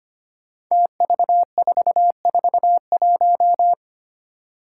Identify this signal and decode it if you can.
Morse code, heard as a keyed tone, about 25 wpm, 700 Hz. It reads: TV441
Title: Five Challenging Letter-Number Combinations 25wpm